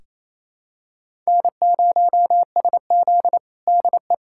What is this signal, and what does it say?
Morse code, heard as a keyed tone, about 28 wpm, 700 Hz. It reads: N0SZ DE
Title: USA Callsign DE Callsign - 4 Characters - 28wpm